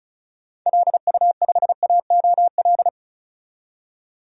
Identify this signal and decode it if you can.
Morse code, heard as a keyed tone, about 36 wpm, 700 Hz. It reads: LU5AOL
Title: DXCC-100 Call Signs - 6 Characters - 35wpm